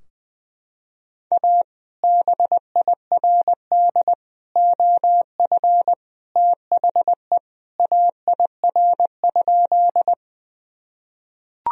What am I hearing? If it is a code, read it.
Morse code, heard as a keyed tone, about 20 wpm, 700 Hz. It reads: A BIRD OF THE AIR?
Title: Sentences from Top 1000 Words - Review Part 2 - 20wpm